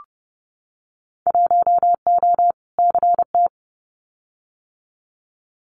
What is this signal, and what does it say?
Morse code, heard as a keyed tone, about 30 wpm, 700 Hz. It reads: 1O CT
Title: Winter Field Day Exchanges 30wpm